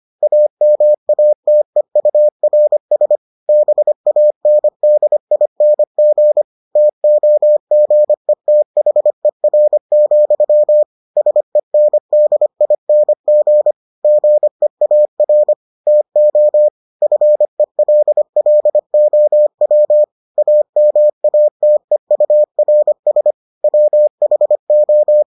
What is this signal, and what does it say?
Morse code, heard as a keyed tone, about 25 wpm, 600 Hz. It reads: AMATEURS BANDING TOGETHER, SENDING GEAR TO FELLOW AMATEURS WHO